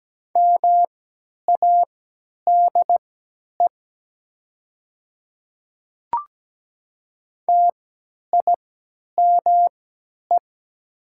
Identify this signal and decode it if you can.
Morse code, heard as a keyed tone, about 17 wpm, spaced as for 10 wpm, 700 Hz. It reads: MADE TIME
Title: 3x Character Spacing - Top 200 Words - 17wpm